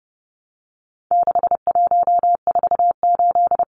Morse code, heard as a keyed tone, about 30 wpm, 700 Hz. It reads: 6148
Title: Numbers - 4-Digit 30wpm